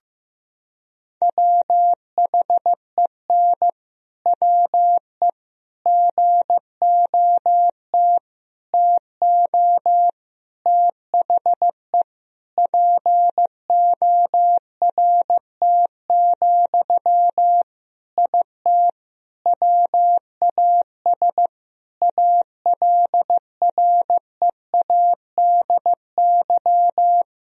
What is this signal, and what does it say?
Morse code, heard as a keyed tone, about 15 wpm, 700 Hz. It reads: WHEN WE GOT TO THE PORT, IT WAS ALREADY
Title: Sentences from Top 1000 Words - Review Part 1 - 15wpm